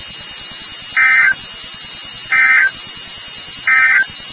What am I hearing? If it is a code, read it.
Emergency Alert System (SAME) tones, heard as harsh buzzing bursts. It is heard from 0.9 to 4.1 s.
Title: NOAA Weather Radio WXK36